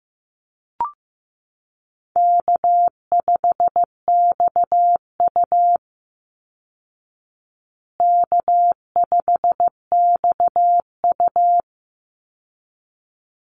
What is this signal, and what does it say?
Morse code, heard as a keyed tone, about 15 wpm, 700 Hz. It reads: K5XU K5XU
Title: Call Signs Repeated 3x - 15wpm